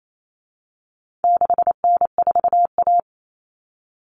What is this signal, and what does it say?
Morse code, heard as a keyed tone, about 28 wpm, 700 Hz. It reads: 6N4A
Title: Common Cut Numbers - 4-Digit 28wpm